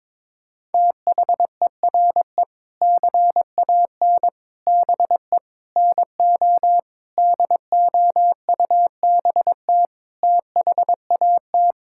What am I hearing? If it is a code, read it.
Morse code, heard as a keyed tone, about 22 wpm, 700 Hz. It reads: THERE CAN BE NO DOUBT THAT
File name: practice / 22wpm / Sentences with 7-Word Phrases - 22wpm